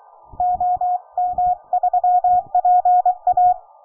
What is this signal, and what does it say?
Morse code, heard as a keyed tone, about 24 wpm, 720 Hz. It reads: OM3PA